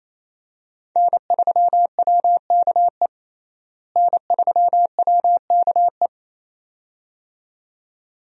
Morse code, heard as a keyed tone, about 28 wpm, 700 Hz. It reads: N3WKE N3WKE
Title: USA Call Signs - 5 Characters - Repeated 2x - 28wpm